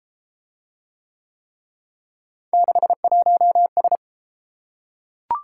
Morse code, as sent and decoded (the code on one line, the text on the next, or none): -.... .---- ...
61S